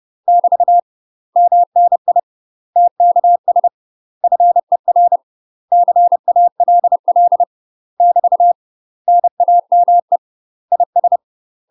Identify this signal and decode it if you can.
Morse code, heard as a keyed tone, about 30 wpm, 700 Hz. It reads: <BT> MNI TKS FER CALL <BT> NAME IS
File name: UK MASTER QSOs TRAINING / 30WPM / UK MASTER QSOs 0007